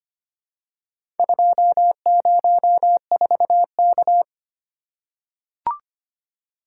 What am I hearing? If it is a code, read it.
Morse code, heard as a keyed tone, about 25 wpm, 700 Hz. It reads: 204K
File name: practice / 25wpm / Four Letter-Number Combinations - 25wpm